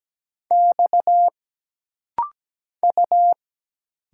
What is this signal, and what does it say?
Morse code, heard as a keyed tone, about 17 wpm, 700 Hz. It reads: X U
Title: Single Letter-Number - Rapid-Fire 17wpm